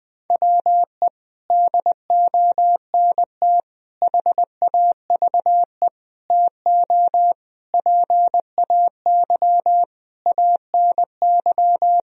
Morse code, heard as a keyed tone, about 21 wpm, 700 Hz. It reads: WE DONT HAVE TO PAY ANY
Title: Sentences from Top 1000 Words - Review Part 2 - 20wpm